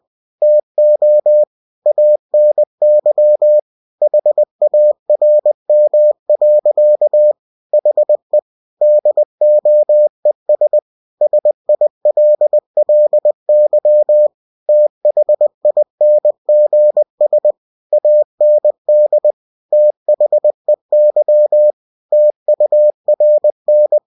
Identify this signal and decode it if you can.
Morse code, heard as a keyed tone, about 20 wpm, 600 Hz. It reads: TO ANY HARM. HE DOES SILLY THINGS AND THEY TURN
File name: cw-books / winnie-the-pooh / winnie-20wpm-page-0091